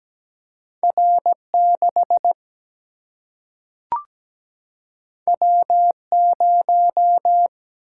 Morse code, heard as a keyed tone, about 17 wpm, 700 Hz. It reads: R6 W0
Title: Single-Letter Single-Number 17wpm